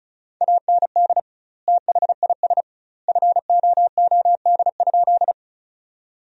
Morse code, heard as a keyed tone, about 34 wpm, 700 Hz. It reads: AND THIS FOOD?
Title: 2x Word-Spacing - Sentences from Top 300 Words - 35wpm